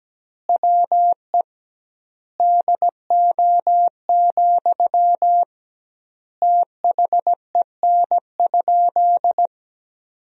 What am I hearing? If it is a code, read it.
Morse code, heard as a keyed tone, about 17 wpm, 700 Hz. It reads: WE DO, THEN?
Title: 2x Word-Spacing - Sentences from Top 100 Words - 17wpm